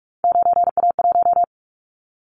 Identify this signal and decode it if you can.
Morse code, heard as a keyed tone, about 46 wpm, 700 Hz. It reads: 9A1
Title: Callsign Prefixes for Top 50 DXCC - 45wpm